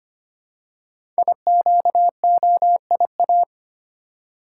Morse code, heard as a keyed tone, about 25 wpm, 700 Hz. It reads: IQOIA
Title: Code Groups - 5 Letters - 25wpm